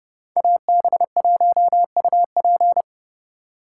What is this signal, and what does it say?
Morse code, heard as a keyed tone, about 30 wpm, 700 Hz. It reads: AB1UP